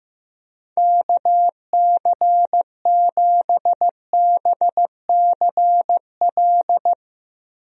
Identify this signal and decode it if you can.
Morse code, heard as a keyed tone, about 15 wpm, 700 Hz. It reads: KC7BCL
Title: USA Call Signs - 6 Characters - 15wpm